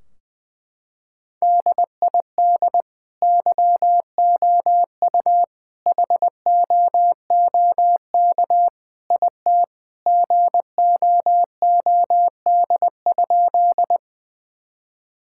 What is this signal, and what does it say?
Morse code, heard as a keyed tone, about 21 wpm, 700 Hz. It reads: DID YOU HOOK IT GOOD?
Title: Sentences from Top 1600 Words 20wpm